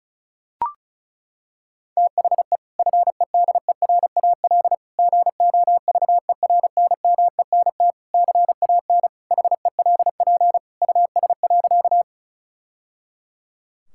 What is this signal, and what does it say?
Morse code, heard as a keyed tone, about 34 wpm, 700 Hz. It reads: THE FEDERAL GOVERNMENT CAN HELP US.
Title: Sentences from Top 1000 Words - Review Part 5 - 35wpm